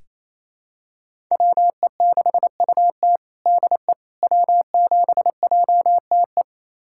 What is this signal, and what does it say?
Morse code, heard as a keyed tone, about 28 wpm, 700 Hz. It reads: WE6UT DE W7JTE
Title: USA Callsign DE Callsign - 5 Characters - 28wpm